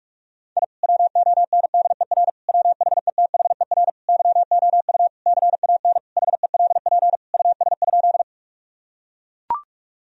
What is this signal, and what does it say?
Morse code, heard as a keyed tone, about 41 wpm, 700 Hz. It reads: I WONDER WHETHER YOU CAN HELP US?
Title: Sentences from Top 1000 Words - Review Part 3 - 45wpm